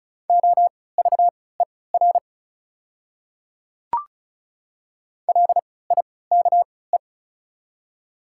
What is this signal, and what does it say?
Morse code, heard as a keyed tone, about 36 wpm, spaced as for 20 wpm, 700 Hz. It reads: OVER LIKE
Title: 3x Character Spacing - Top 100 Words - 35wpm